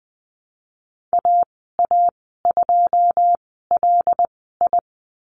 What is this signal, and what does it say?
Morse code, heard as a keyed tone, about 20 wpm, spaced as for 15 wpm, 700 Hz. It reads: AA2LI
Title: 167 Lesson 27g - Number 7 - Call Signs 20wpm